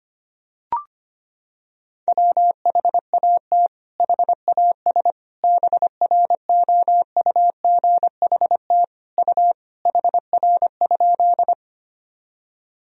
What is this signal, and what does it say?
Morse code, heard as a keyed tone, about 25 wpm, 700 Hz. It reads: WHAT HAS BROUGHT U HR?
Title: Sentences from Top 500 Words and Abbreviations 25wpm